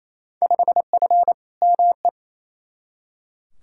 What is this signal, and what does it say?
Morse code, heard as a keyed tone, about 28 wpm, 700 Hz. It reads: 5F ME
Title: Field Day Exchanges Encore - 28wpm